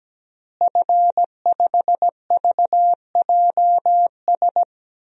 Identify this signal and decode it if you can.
Morse code, heard as a keyed tone, about 17 wpm, 700 Hz. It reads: F5VJS
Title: DXCC-100 Call Signs - 5 Characters - Repeated 2x - 17wpm